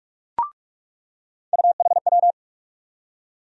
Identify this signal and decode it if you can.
Morse code, heard as a keyed tone, about 44 wpm, 700 Hz. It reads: UHW
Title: USA Call Sign Postfixes - 3 Letters 45wpm